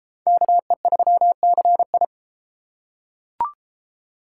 Morse code, heard as a keyed tone, about 32 wpm, 700 Hz. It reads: KE3CI